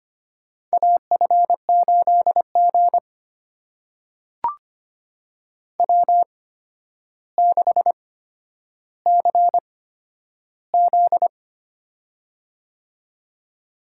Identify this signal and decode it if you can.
Morse code, heard as a keyed tone, about 25 wpm, spaced as for 11 wpm, 700 Hz. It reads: AF8G W6CZ
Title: USA Call Signs - 4 Characters - 8x Character Spacing - 25wpm